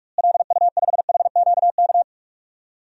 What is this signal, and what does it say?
Morse code, heard as a keyed tone, about 44 wpm, 700 Hz. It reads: LU5HQK
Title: DXCC-100 Call Signs - 6 Characters - Repeated 2x - 45wpm